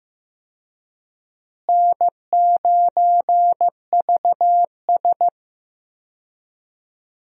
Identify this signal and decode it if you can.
Morse code, heard as a keyed tone, about 15 wpm, 700 Hz. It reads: N9VS
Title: USA Call Signs - 4 Characters - Repeated 2x - 15wpm